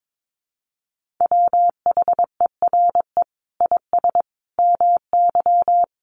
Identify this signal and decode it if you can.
Morse code, heard as a keyed tone, about 22 wpm, 700 Hz. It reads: WHERE IS MY
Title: Sentences from Top 1400 Words 22wpm